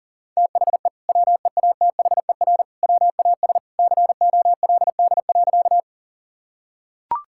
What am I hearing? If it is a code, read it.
Morse code, heard as a keyed tone, about 40 wpm, 700 Hz. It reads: THE WEATHER WAS COLD.
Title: Sentences from Top 1000 Words - Review Part 5 - 40wpm